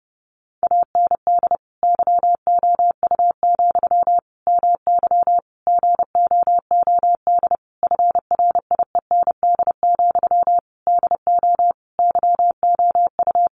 Morse code, heard as a keyed tone, about 30 wpm, 700 Hz. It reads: AND YOU, MY GOOD FRIEND, DO YOU
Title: Sentences from Top 1100 Words 30wpm